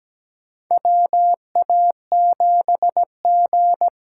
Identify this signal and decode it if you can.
Morse code, heard as a keyed tone, about 17 wpm, 700 Hz. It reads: WA7G
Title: USA Call Signs - 6 Characters - 8x Character Spacing - 17wpm